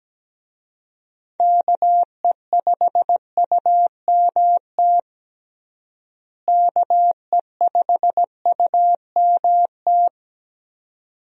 Morse code, heard as a keyed tone, about 17 wpm, 700 Hz. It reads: KE5UMT KE5UMT
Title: USA Call Signs - 6 Characters - Repeated 2x - 17wpm